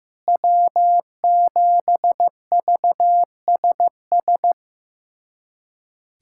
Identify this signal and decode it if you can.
Morse code, heard as a keyed tone, about 15 wpm, 700 Hz. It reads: W7VSS
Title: USA Callsign DE Callsign - 5 Characters - 15wpm